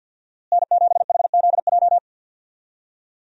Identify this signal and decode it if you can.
Morse code, heard as a keyed tone, about 52 wpm, 690 Hz. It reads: N7HZJ
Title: USA Call Signs - 5 Characters - Repeated 3x - 50wpm